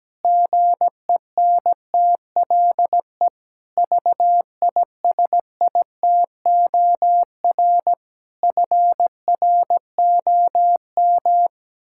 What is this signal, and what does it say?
Morse code, heard as a keyed tone, about 17 wpm, 700 Hz. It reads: GENTLE VISITOR FROM
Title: Pangram Challenge v2 - 17wpm